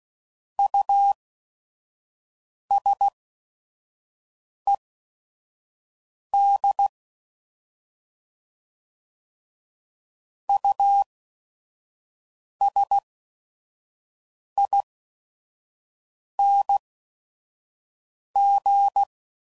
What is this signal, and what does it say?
Morse code, heard as a keyed tone, about 16 wpm, spaced as for 5 wpm, 800 Hz. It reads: USED USING
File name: Prefixes / By First Letter / Us